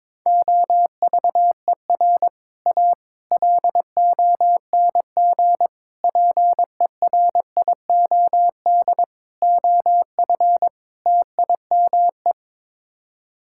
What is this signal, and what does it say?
Morse code, heard as a keyed tone, about 22 wpm, 700 Hz. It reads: OVER A LONG PERIOD OF TIME
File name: practice / 22wpm / Sets of 6 Words - 22wpm